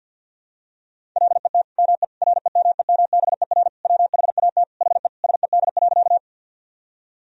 Morse code, heard as a keyed tone, about 48 wpm, 700 Hz. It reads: LET ME REMEMBER WHAT HE SED.
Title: Sentences from Top 400 Words and Abbreviations 50wpm